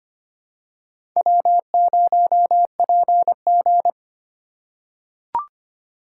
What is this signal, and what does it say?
Morse code, heard as a keyed tone, about 25 wpm, 700 Hz. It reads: W0PG
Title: USA Call Signs - 4 Characters - 2x Character Spacing - 25wpm